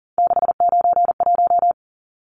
Morse code, heard as a keyed tone, about 40 wpm, 700 Hz. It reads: 691
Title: Numbers - 3-Digit 40wpm